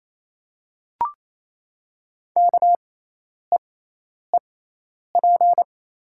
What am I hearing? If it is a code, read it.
Morse code, heard as a keyed tone, about 28 wpm, spaced as for 10 wpm, 700 Hz. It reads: KEEP
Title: 6x Character Spacing - Top 300 Words - 28wpm